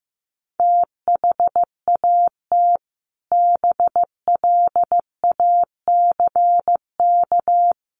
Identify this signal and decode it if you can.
Morse code, heard as a keyed tone, about 15 wpm, 700 Hz. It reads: THAT BLACK
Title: Sentences from Top 1400 Words 15wpm